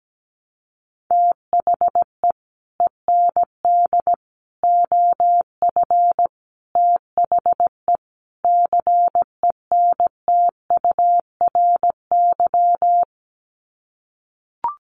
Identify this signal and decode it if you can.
Morse code, heard as a keyed tone, about 17 wpm, 700 Hz. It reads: THE END OF THE CENTURY
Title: Sets of 5 Words Encore2 17wpm